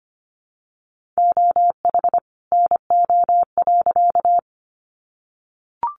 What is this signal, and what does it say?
Morse code, heard as a keyed tone, about 25 wpm, 700 Hz. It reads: OH NO.
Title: Additional Easy Phrases 25wpm